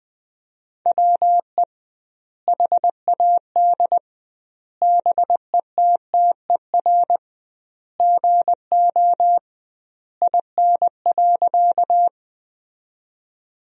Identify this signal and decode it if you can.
Morse code, heard as a keyed tone, about 20 wpm, 700 Hz. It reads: WE HAD BETTER GO IN.
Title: 2x Word-Spacing - Sentences from Top 400 Words - 20wpm